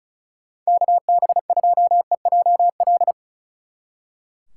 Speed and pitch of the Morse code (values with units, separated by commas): 45 wpm, 700 Hz